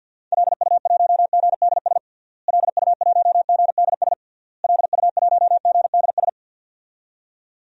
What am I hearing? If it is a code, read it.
Morse code, heard as a keyed tone, about 52 wpm, 700 Hz. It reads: LU1GDS LU1GDS LU1GDS
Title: DXCC-100 Call Signs - 6 Characters - Repeated 3x - 50wpm